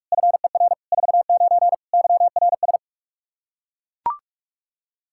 Morse code, heard as a keyed tone, about 41 wpm, 700 Hz. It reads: FER 49 YRS
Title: Practice QSO - Exchange 3 - More Info - 45wpm